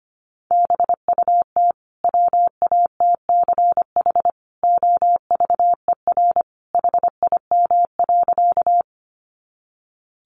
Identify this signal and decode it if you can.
Morse code, heard as a keyed tone, about 25 wpm, 700 Hz. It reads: BUT WATCH OVER HIM.
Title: Sentences from Top 300 Words 25wpm